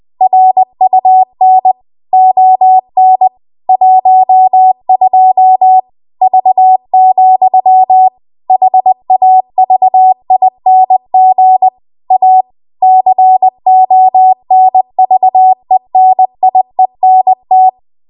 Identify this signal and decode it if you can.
Morse code, heard as a keyed tone, about 20 wpm, 750 Hz. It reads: RUN ON 12 V, HAVING A CONVENIENT
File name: Archive / 20 WPM / 20WPM